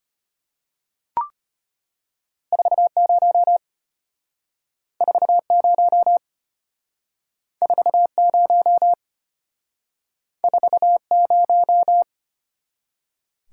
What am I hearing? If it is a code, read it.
Morse code, heard as a keyed tone, about 33 wpm, 700 Hz. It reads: 40 40 40 40